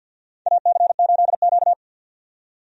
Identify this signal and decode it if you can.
Morse code, heard as a keyed tone, about 51 wpm, 700 Hz. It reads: AC8Q
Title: USA Call Signs - 4 Characters - 50wpm